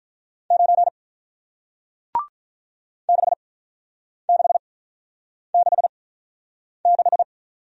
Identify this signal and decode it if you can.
Morse code, heard as a keyed tone, about 47 wpm, spaced as for 9 wpm, 700 Hz. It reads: 9 6666